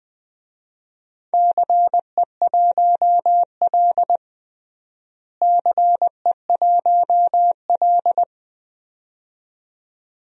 Morse code, heard as a keyed tone, about 21 wpm, 700 Hz. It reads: CE1L CE1L
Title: DXCC-100 Call Signs - 4 Characters - Repeated 2x - 20wpm